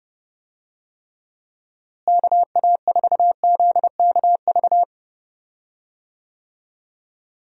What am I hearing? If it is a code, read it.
Morse code, heard as a keyed tone, about 31 wpm, 700 Hz. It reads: KA4ZKV
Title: USA Call Signs - 6 Characters - Repeated 3x - 30wpm